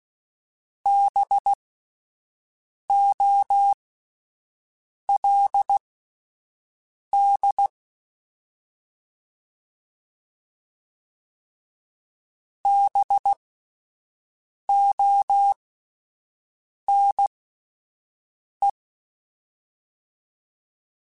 Morse code, heard as a keyed tone, about 16 wpm, spaced as for 5 wpm, 800 Hz. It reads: BOLD BONE
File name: Spaced Alphas / Spaced Alphas mp3s / Bs sp @10